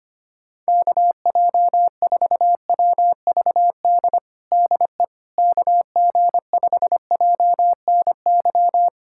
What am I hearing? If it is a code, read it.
Morse code, heard as a keyed tone, about 25 wpm, 700 Hz. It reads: KJ4WVD DE KG5JNY
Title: USA Callsign DE Callsign - 6 Characters - 25wpm